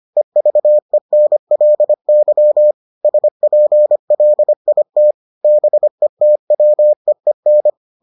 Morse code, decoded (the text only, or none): EVENLY SPLIT BETWEEN